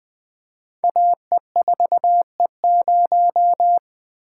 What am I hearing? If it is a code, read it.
Morse code, heard as a keyed tone, about 21 wpm, 700 Hz. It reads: AE4E0
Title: Code Groups - 5 Letter:Numbers - 20wpm